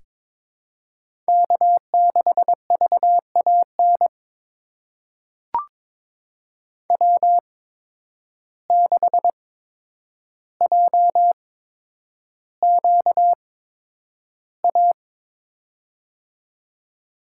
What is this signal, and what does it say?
Morse code, heard as a keyed tone, about 22 wpm, spaced as for 9 wpm, 700 Hz. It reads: K6VAN W6JQA
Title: USA Call Signs - 5 Characters - 8x Character Spacing - 22wpm